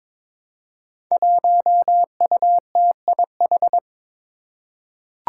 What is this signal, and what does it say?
Morse code, heard as a keyed tone, about 22 wpm, 700 Hz. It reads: 1UTIH
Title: Five Challenging Letter-Number Combinations 22wpm